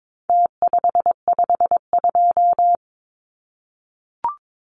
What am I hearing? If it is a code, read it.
Morse code, heard as a keyed tone, about 22 wpm, 700 Hz. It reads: T552
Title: Numbers - Contest Style - 4-Digit 22wpm